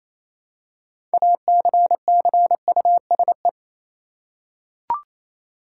Morse code, heard as a keyed tone, about 28 wpm, 700 Hz. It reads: ACCUSE
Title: Six Letter Words - 28wpm